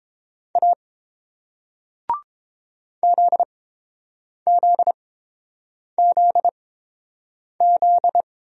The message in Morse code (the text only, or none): A ZZZZ